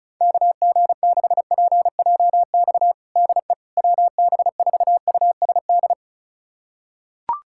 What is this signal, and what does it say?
Morse code, heard as a keyed tone, about 36 wpm, 700 Hz. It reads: KG6PJX DE WB4USD
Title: USA Callsign DE Callsign - 6 Characters - 35wpm